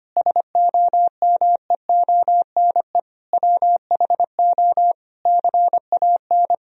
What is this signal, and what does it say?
Morse code, heard as a keyed tone, about 25 wpm, 700 Hz. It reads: SOMEONE WHO CAN
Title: Sentences from Top 1000 Words - Review Part 1 - 25wpm